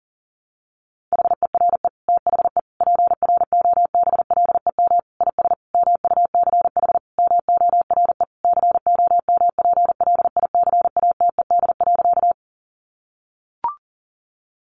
Morse code, heard as a keyed tone, about 40 wpm, 700 Hz. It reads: HERE THE PROBLEM IS MUCH MORE COMPLICATED.
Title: Sentences from Top 1200 Words 40wpm